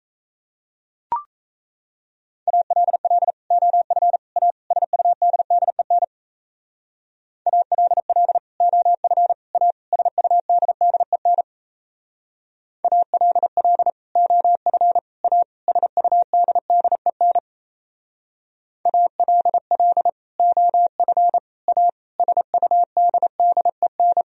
Morse code, heard as a keyed tone, about 34 wpm, 700 Hz. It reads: ALL OF A SUDDEN ALL OF A SUDDEN ALL OF A SUDDEN ALL OF A SUDDEN
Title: Speed-Racing - Sets of 4 Words 28wpm